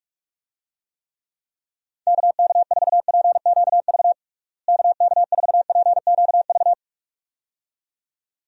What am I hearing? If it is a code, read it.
Morse code, heard as a keyed tone, about 46 wpm, 700 Hz. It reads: KK4PQV KK4PQV
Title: USA Call Signs - 6 Characters - Repeated 2x - 45wpm